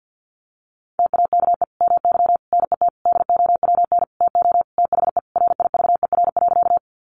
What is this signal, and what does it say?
Morse code, heard as a keyed tone, about 50 wpm, 700 Hz. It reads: TAKE MY NET DOWN TO THE RIVER.
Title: Sentences from Top 1400 Words 50wpm